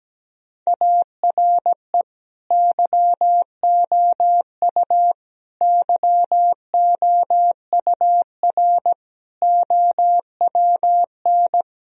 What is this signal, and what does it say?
Morse code, heard as a keyed tone, about 17 wpm, 700 Hz. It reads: ARE YOU YOUR OWN
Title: Sentences from Top 1000 Words - Review Part 4 - 17wpm